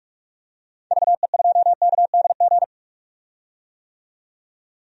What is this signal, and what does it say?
Morse code, heard as a keyed tone, about 46 wpm, 700 Hz. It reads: VE2KDG
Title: DXCC-100 Call Signs - 6 Characters - 45wpm